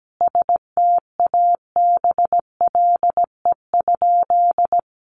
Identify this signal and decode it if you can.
Morse code, heard as a keyed tone, about 17 wpm, 700 Hz. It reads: STABLE?
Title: Sentences from Top 1500 Words 17wpm